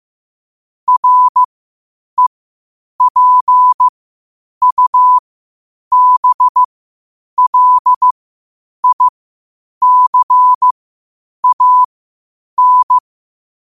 Morse code, heard as a keyed tone, about 15 wpm, spaced as for 8 wpm, 1000 Hz. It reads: REPUBLICAN
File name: qotdfiles / 8wpm